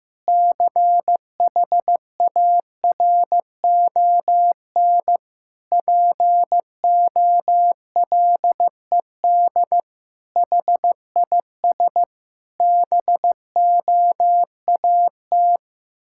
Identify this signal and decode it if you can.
Morse code, heard as a keyed tone, about 15 wpm, 700 Hz. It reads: CHARON POLED HIS BOAT
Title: Pangram Challenge v2 - 15wpm